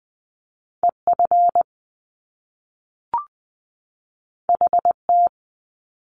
Morse code, heard as a keyed tone, about 20 wpm, 700 Hz. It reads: E F H T
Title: Top Two Letter Combinations in Words 20wpm